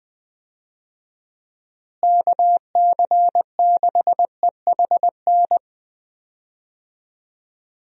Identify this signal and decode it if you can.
Morse code, heard as a keyed tone, about 20 wpm, 700 Hz. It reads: KC6EHN
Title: USA Call Signs - 6 Characters - Repeated 3x - 20wpm